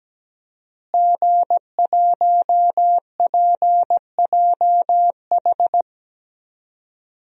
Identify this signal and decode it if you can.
Morse code, heard as a keyed tone, about 17 wpm, 700 Hz. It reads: G1PJH
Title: DXCC-100 Call Signs - 5 Characters - Repeated 3x - 17wpm